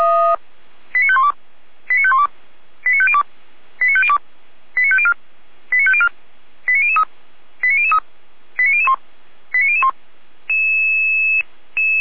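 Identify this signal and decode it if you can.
Five-tone selective call (ZVEI), heard as quick runs of five tones, tones of about 70 ms. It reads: 89512 89512 89502 89502 89504 89504 89003 89003 89001 89001
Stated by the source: ZVEI five-tone selective call 89160 received by SDR, fire brigade Feuerwehr Orlamünde.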